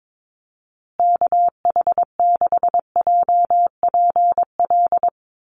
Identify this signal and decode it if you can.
Morse code, heard as a keyed tone, about 22 wpm, 690 Hz. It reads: KH6JPL
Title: USA Call Signs Repeated 3x 22wpm